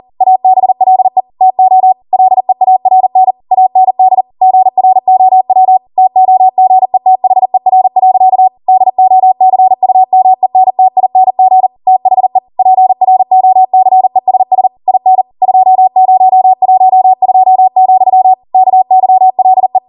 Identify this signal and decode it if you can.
Morse code, heard as a keyed tone, about 40 wpm, 750 Hz. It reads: ABLE TO LEARN AND GROW TOGETHER. DOCUMENTING THE PROCESS IN 2012, KYLE